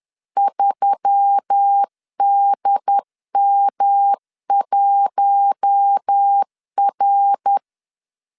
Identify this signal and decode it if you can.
Morse code, heard as a keyed tone, about 10 wpm, 800 Hz. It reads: 3DM1R